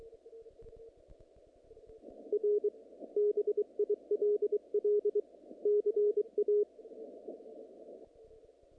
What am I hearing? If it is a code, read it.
Morse code, heard as a keyed tone, about 26 wpm, 420 Hz. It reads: R BILL CA
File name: contests / naqp-cw-k5tr-op-k5pi- -2200